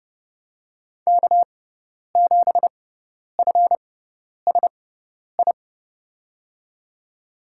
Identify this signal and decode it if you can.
Morse code, heard as a keyed tone, about 32 wpm, spaced as for 10 wpm, 700 Hz. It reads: K7FSI
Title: USA Call Signs - 5 Characters - 6x Character Spacing - 30wpm